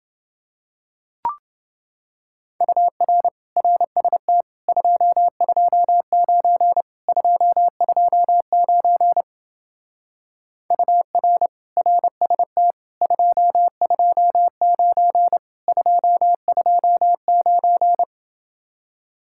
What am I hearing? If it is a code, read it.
Morse code, heard as a keyed tone, about 28 wpm, 700 Hz. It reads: UR RST 229 229 UR RST 229 229